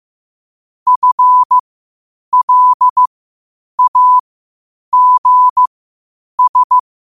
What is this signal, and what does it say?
Morse code, heard as a keyed tone, about 15 wpm, spaced as for 8 wpm, 1000 Hz. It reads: FLAGS
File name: qotdfiles / 8wpm